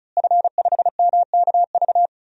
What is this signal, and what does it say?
Morse code, heard as a keyed tone, about 36 wpm, 700 Hz. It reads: F5MKV
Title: DXCC-100 Call Signs - 5 Characters - Repeated 3x - 35wpm